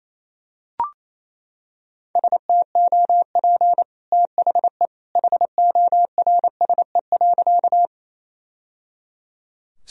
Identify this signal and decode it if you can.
Morse code, heard as a keyed tone, about 28 wpm, 700 Hz. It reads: STOP THE HORSE.
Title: Sentences from Top 1000 Words - Review Part 1 - 28wpm